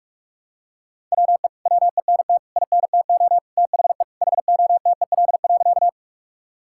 Morse code, heard as a keyed tone, about 46 wpm, 700 Hz. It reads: WE WENT INTO THE HOTEL.
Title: Sentences from Top 1600 Words 45wpm